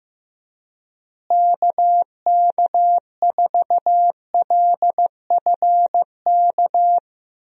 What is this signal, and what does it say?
Morse code, heard as a keyed tone, about 15 wpm, 700 Hz. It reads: KK4LFK
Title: USA Call Signs - 6 Characters - Repeated 3x - 15wpm